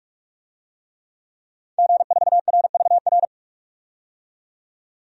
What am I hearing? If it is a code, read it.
Morse code, heard as a keyed tone, about 43 wpm, 700 Hz. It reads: G4RVR
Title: DXCC-100 Call Signs - 5 Characters - 45wpm